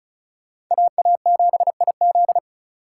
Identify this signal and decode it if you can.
Morse code, heard as a keyed tone, about 46 wpm, 700 Hz. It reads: AA7IZ